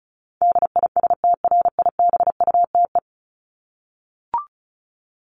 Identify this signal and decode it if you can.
Morse code, heard as a keyed tone, about 34 wpm, 700 Hz. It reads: DISTRIBUTE
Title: Top 1501-1750 Words 35wpm